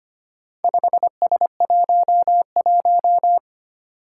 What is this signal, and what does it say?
Morse code, heard as a keyed tone, about 25 wpm, 700 Hz. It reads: 5S11
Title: Four Challenging Letter-Number Combinations 25wpm